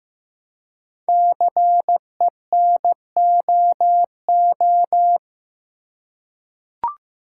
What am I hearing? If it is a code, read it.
Morse code, heard as a keyed tone, about 15 wpm, 700 Hz. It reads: CENOO
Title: Code Groups - 5 Letters - 15wpm